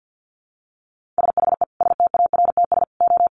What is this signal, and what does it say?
Morse code, heard as a keyed tone, about 50 wpm, 700 Hz. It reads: SHE STARTS O
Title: Easy Phrases 50wpm